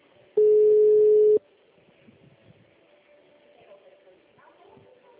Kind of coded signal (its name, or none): telephone call progress tone